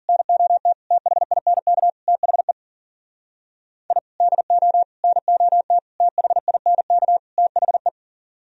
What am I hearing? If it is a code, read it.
Morse code, heard as a keyed tone, about 54 wpm, 700 Hz. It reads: NOT THINK THE I DO NOT THINK THE